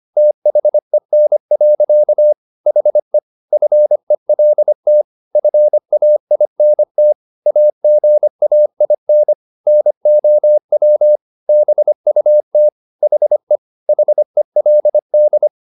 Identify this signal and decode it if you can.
Morse code, heard as a keyed tone, about 25 wpm, 600 Hz. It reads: THEN. HE FELT FAINT AGAIN NOW BUT HE HELD